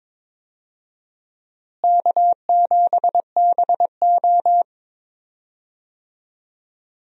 Morse code, heard as a keyed tone, about 22 wpm, 700 Hz. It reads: K7BO
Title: USA Call Signs - 4 Characters - Repeated 3x - 22wpm